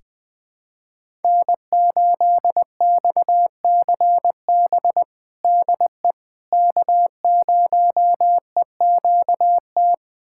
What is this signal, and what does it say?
Morse code, heard as a keyed tone, about 20 wpm, 700 Hz. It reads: N8XCB DE K0EQT
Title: USA Callsign DE Callsign - 5 Characters - 20wpm